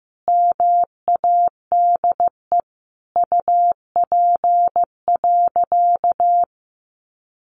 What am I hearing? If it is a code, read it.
Morse code, heard as a keyed tone, about 15 wpm, 700 Hz. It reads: MADE UP.
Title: Sentences from Top 500 Words 15wpm